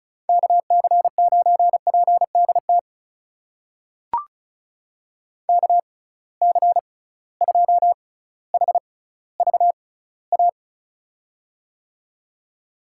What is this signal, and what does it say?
Morse code, heard as a keyed tone, about 36 wpm, spaced as for 18 wpm, 700 Hz. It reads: KC9PDT KC2HVA
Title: USA Call Signs - 6 Characters - 6x Character Spacing - 35wpm